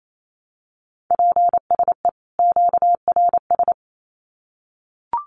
Morse code, heard as a keyed tone, about 28 wpm, 700 Hz. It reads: PSE QRS
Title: Common QSO Phrases 28wpm